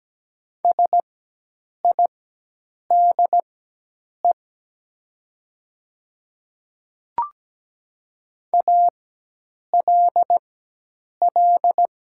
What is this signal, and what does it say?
Morse code, heard as a keyed tone, about 17 wpm, spaced as for 8 wpm, 700 Hz. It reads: SIDE ALL
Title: 4x Character Spacing - Top 100 Words - 17wpm